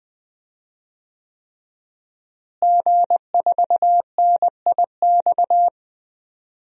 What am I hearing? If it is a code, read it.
Morse code, heard as a keyed tone, about 21 wpm, 700 Hz. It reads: G4NIX
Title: DXCC-100 Call Signs - 5 Characters - Repeated 3x - 20wpm